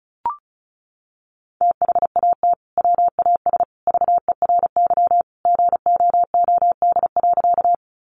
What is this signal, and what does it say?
Morse code, heard as a keyed tone, about 34 wpm, 700 Hz. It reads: THAT WAS VERY GOOD.
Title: Sentences from Top 200 Words 35wpm